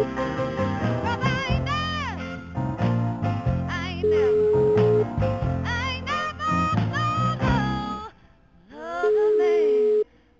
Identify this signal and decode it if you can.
Ringback tone (Europe, 425 Hz), with noise.